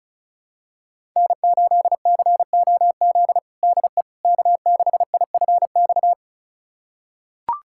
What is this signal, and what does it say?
Morse code, heard as a keyed tone, about 34 wpm, 700 Hz. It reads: N8COZ DE K6IFX
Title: USA Callsign DE Callsign - 5 Characters - 35wpm